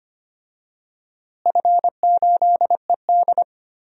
Morse code, heard as a keyed tone, about 25 wpm, 700 Hz. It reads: F8ED
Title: DXCC-100 Call Signs - 4 Characters - Repeated 3x - 25wpm